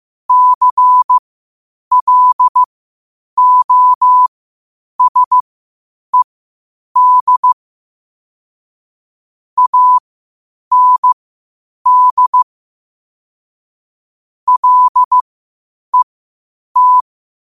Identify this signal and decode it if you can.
Morse code, heard as a keyed tone, about 15 wpm, spaced as for 8 wpm, 1000 Hz. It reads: CLOSED AND LET
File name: qotdfiles / 8wpm